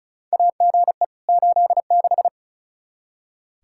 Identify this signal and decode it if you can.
Morse code, heard as a keyed tone, about 36 wpm, 700 Hz. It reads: AGE 86
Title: K1USN SST QSO Practice 35wpm